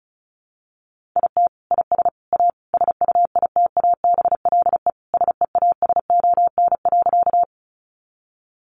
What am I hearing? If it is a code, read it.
Morse code, heard as a keyed tone, about 34 wpm, 700 Hz. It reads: IT IS A SUITABLE SEASON.
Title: Sentences from Top 1400 Words 35wpm